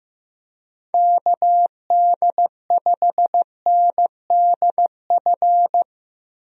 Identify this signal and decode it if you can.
Morse code, heard as a keyed tone, about 15 wpm, 700 Hz. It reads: KD5NDF
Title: USA Call Signs - 6 Characters - 15wpm